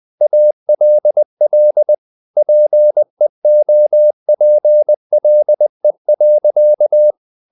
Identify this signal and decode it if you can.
Morse code, heard as a keyed tone, about 20 wpm, 600 Hz. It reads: ALL PEOPLE.
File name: the old man and the sea / the old man and the sea part 1 20wpm